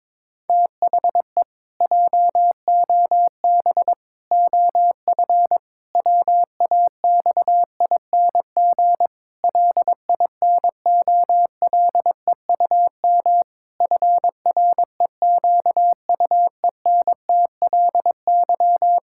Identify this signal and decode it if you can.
Morse code, heard as a keyed tone, about 22 wpm, 700 Hz. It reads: THE JOB OF WAXING LINOLEUM FREQUENTLY
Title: Pangram Challenge v2 - 22wpm